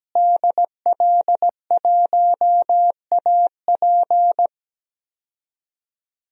Morse code, heard as a keyed tone, about 17 wpm, 700 Hz. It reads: DL1AP
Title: DXCC-100 Call Signs - 5 Characters - Repeated 2x - 17wpm